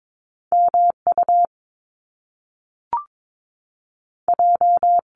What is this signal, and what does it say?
Morse code, heard as a keyed tone, about 22 wpm, spaced as for 3 wpm, 700 Hz. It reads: M UJ
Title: Top Two Letter Combinations in Words 22wpm